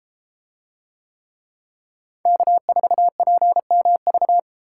Morse code, heard as a keyed tone, about 31 wpm, 700 Hz. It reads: K4PMV